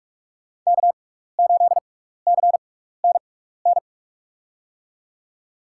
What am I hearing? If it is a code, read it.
Morse code, heard as a keyed tone, about 47 wpm, spaced as for 16 wpm, 700 Hz. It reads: K8CNN
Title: USA Call Signs - 5 Characters - 6x Character Spacing - 45wpm